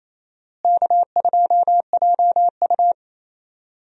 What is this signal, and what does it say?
Morse code, heard as a keyed tone, about 28 wpm, 700 Hz. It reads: K2JU
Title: USA Call Signs - 4 Characters - Repeated 2x - 28wpm